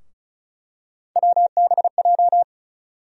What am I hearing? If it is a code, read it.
Morse code, heard as a keyed tone, about 36 wpm, 700 Hz. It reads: WBJ
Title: USA Call Sign Postfixes - 3 Letters 35wpm